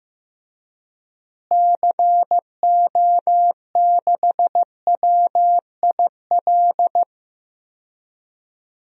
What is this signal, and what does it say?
Morse code, heard as a keyed tone, about 15 wpm, 700 Hz. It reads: CO6WIL
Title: DXCC-100 Call Signs - 6 Characters - Repeated 2x - 15wpm